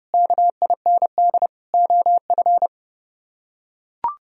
Morse code, heard as a keyed tone, about 31 wpm, 700 Hz. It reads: KIND OF